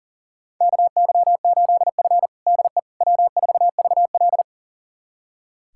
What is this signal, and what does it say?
Morse code, heard as a keyed tone, about 40 wpm, 700 Hz. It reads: KY8F DE W4VL
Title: USA Callsign DE Callsign - 4 Characters - 40wpm